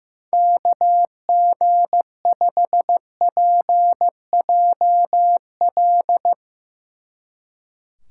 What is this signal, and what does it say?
Morse code, heard as a keyed tone, about 15 wpm, 700 Hz. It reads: KG5PJL
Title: Speed Racing - USA 2x3 Call Signs 15wpm